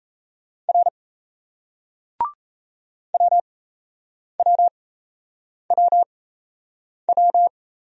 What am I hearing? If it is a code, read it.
Morse code, heard as a keyed tone, about 36 wpm, spaced as for 8 wpm, 700 Hz. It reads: R WWWW